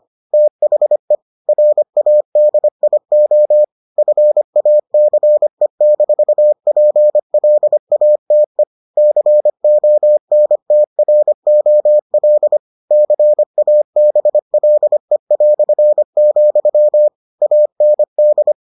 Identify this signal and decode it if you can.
Morse code, heard as a keyed tone, about 25 wpm, 600 Hz. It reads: THE RADIO FACE-PLATE CONTROL CABLE", AND